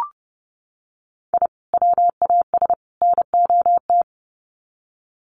Morse code, heard as a keyed tone, about 30 wpm, 700 Hz. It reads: I WAS NOT
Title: Sets of 3 Words 30wpm